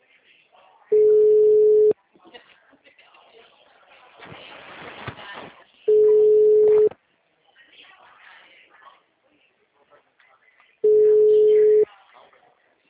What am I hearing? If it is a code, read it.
Ringback tone (Europe, 425 Hz), with speech.
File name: out- -2025